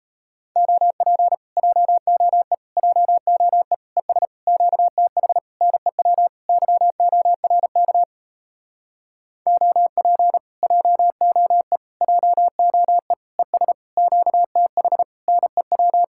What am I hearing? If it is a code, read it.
Morse code, heard as a keyed tone, about 37 wpm, 700 Hz. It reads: OP JOE JOE ES QTH NEW YORK OP JOE JOE ES QTH NEW